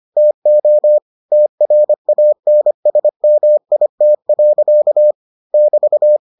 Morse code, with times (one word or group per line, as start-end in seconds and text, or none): 0.2-1.0 TO
1.3-5.1 TRANSMIT.
5.5-6.2 <BT>